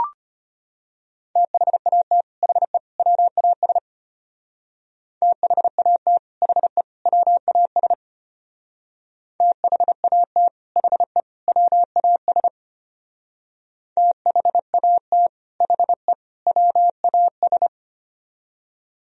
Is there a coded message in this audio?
Yes: Morse code from 1.4 to 17.7 s, heard as a keyed tone, about 32 wpm, 700 Hz. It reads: THAT HE WAS THAT HE WAS THAT HE WAS THAT HE WAS